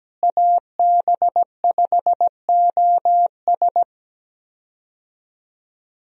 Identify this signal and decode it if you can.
Morse code, heard as a keyed tone, about 17 wpm, 700 Hz. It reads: AB5OS
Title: USA Call Signs - 5 Characters - Repeated 2x - 17wpm